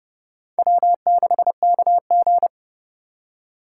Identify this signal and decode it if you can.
Morse code, heard as a keyed tone, about 36 wpm, 700 Hz. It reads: W6KG
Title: Speed Racing - USA 1x2 Call Signs 30wpm